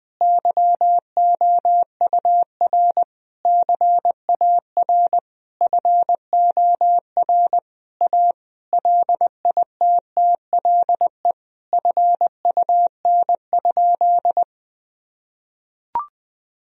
Morse code, heard as a keyed tone, about 21 wpm, 700 Hz. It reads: YOUR CAR FOR A LITTLE FUN?
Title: Sentences from Top 1000 Words - Review Part 1 - 20wpm